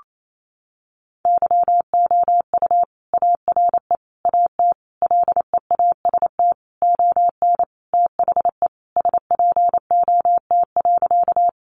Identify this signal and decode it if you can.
Morse code, heard as a keyed tone, about 28 wpm, 700 Hz. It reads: YOU ARE AT LEAST ON THE SPOT.
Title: Sentences from Top 1100 Words 28wpm